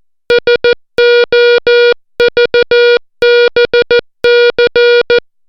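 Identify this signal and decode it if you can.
Morse code, heard as a keyed tone, about 14 wpm, 480 Hz. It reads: SOVBC
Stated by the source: Morse alphabet lesson, 60 characters per minute.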